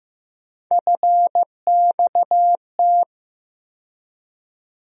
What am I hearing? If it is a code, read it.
Morse code, heard as a keyed tone, about 15 wpm, 700 Hz. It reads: FXT
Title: USA Call Sign Postfixes - 3 Letters 15wpm